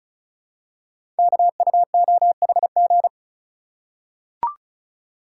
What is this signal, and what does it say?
Morse code, heard as a keyed tone, about 36 wpm, 700 Hz. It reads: KUOHG
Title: Code Groups - 5 Letters - 35wpm